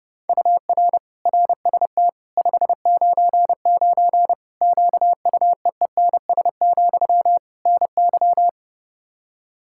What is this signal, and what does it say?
Morse code, heard as a keyed tone, about 32 wpm, 700 Hz. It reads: UR RST 599 QUEENS, NY
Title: Practice QSO - Exchange 1 - Beginning - 30wpm